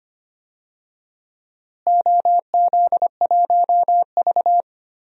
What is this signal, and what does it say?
Morse code, heard as a keyed tone, about 25 wpm, 700 Hz. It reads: OZ1V
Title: DXCC-100 Call Signs - 4 Characters - Repeated 2x - 25wpm